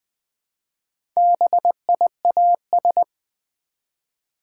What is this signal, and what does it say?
Morse code, heard as a keyed tone, about 20 wpm, 700 Hz. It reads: BIAS
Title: Build A Word - 1 Letter At A Time - 3 to 5-Letter Words - 20wpm